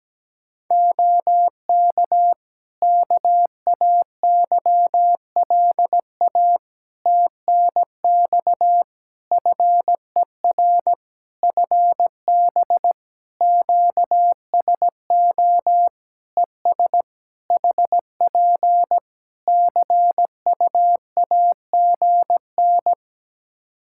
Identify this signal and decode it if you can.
Morse code, heard as a keyed tone, about 17 wpm, 700 Hz. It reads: OK KAYLA TNX FER FB QSO ES HP CUAGN
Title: Practice QSO - Exchange 4 - Ending - 17wpm